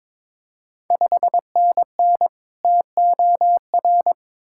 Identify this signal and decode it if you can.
Morse code, heard as a keyed tone, about 22 wpm, 700 Hz. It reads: 5NN TOR
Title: Ontario QSO Party Exchange 22wpm